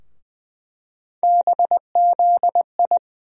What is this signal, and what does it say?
Morse code, heard as a keyed tone, about 20 wpm, 700 Hz. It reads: BZI
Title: USA Call Sign Postfixes - 3 Letters 20wpm